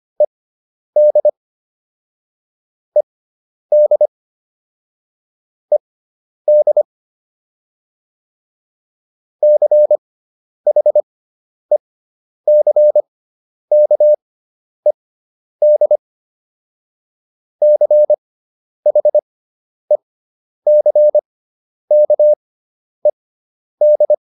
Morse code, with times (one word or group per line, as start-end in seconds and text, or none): 0.2-1.3 ED
3.0-4.1 ED
5.7-6.8 ED
9.4-16.0 CHECKED
17.6-24.1 CHECKED